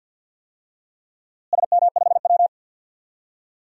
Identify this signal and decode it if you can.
Morse code, heard as a keyed tone, about 47 wpm, 700 Hz. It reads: SM5W
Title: DXCC-100 Call Signs - 4 Characters - Repeated 2x - 50wpm